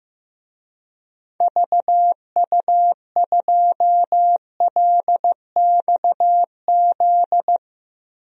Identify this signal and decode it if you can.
Morse code, heard as a keyed tone, about 15 wpm, 700 Hz. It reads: VU2LXZ
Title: DXCC-100 Call Signs - 6 Characters - 15wpm